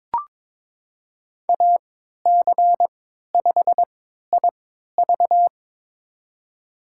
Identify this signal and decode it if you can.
Morse code, heard as a keyed tone, about 22 wpm, spaced as for 13 wpm, 700 Hz. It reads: AC5IV
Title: USA Call Signs - 3x Character Spacing - 22wpm